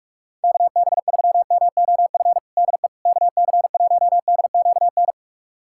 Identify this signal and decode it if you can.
Morse code, heard as a keyed tone, about 46 wpm, 700 Hz. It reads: KB3MOF DE KC1DQN
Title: USA Callsign DE Callsign 45wpm